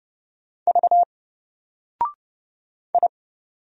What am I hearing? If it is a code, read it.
Morse code, heard as a keyed tone, about 30 wpm, 700 Hz. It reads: V I
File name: Single-Letter-Number-30wpm